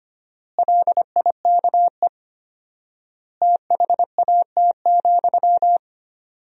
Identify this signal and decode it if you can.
Morse code, heard as a keyed tone, about 25 wpm, 700 Hz. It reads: LIKE THAT,
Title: 4x Word-Spacing - Sentences from Top 400 Words - 25wpm